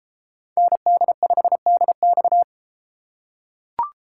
Morse code, heard as a keyed tone, about 32 wpm, 700 Hz. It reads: ND5DX